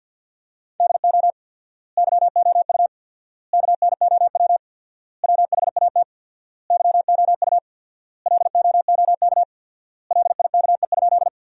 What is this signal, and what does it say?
Morse code, heard as a keyed tone, about 50 wpm, 700 Hz. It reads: DO YOU KNOW WHAT YOU LOOK LIKE?
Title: 4x Word-Spacing - Sentences from Top 100 Words - 50wpm